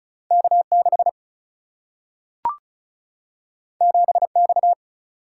Morse code, heard as a keyed tone, about 34 wpm, 700 Hz. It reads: K6 7X
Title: Single-Letter Single-Number 35wpm